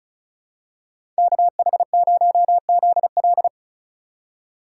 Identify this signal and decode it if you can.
Morse code, heard as a keyed tone, about 36 wpm, 700 Hz. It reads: KH0ZL
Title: USA Call Signs - 5 Characters - 3x Character Spacing - 35wpm